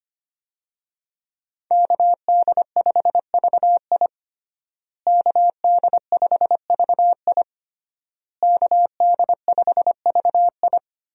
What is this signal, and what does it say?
Morse code, heard as a keyed tone, about 25 wpm, 700 Hz. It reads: KD5VI KD5VI KD5VI
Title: USA Call Signs - 5 Characters - Repeated 3x - 25wpm